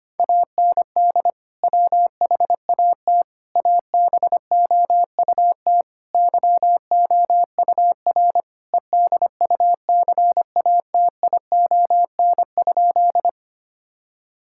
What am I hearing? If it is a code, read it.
Morse code, heard as a keyed tone, about 25 wpm, 700 Hz. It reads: AND WHAT ABOUT YOUR EDUCATION?
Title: Sentences from Top 1000 Words - Review Part 6 - 25wpm